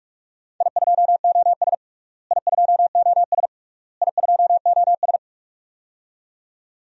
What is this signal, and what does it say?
Morse code, heard as a keyed tone, about 41 wpm, 690 Hz. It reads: I2OS I2OS I2OS
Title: DXCC-100 Call Signs - 4 Characters - Repeated 3x - 45wpm